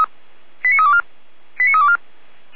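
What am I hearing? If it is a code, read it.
Five-tone selective call (ZVEI), heard as quick runs of five tones, tones of about 70 ms. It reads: 89324 89324
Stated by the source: ZVEI five-tone selective call 89330 received with an SDR.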